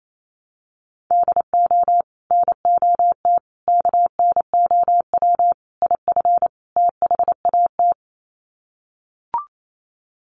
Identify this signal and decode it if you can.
Morse code, heard as a keyed tone, about 28 wpm, 700 Hz. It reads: DO NOT KNOW IF THAT
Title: Sets of 5 Words Encore2 28wpm